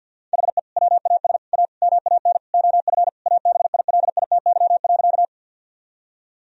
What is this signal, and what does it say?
Morse code, heard as a keyed tone, about 50 wpm, 700 Hz. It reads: HE WAS A MAN OF ABILITY.
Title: Sentences from Top 1000 Words - Review Part 3 - 50wpm